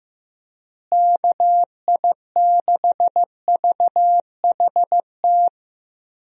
Morse code, heard as a keyed tone, about 15 wpm, 700 Hz. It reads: KI6VHT
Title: USA Call Signs - 6 Characters - Repeated 3x - 15wpm